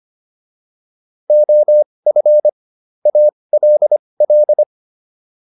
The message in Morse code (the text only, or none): OF ALL